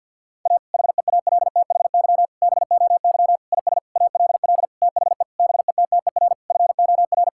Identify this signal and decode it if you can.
Morse code, heard as a keyed tone, about 50 wpm, 700 Hz. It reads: A HEALTHY BOY IS ALL THE BETTER FOR
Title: Sentences from Top 1000 Words - Review Part 5 - 50wpm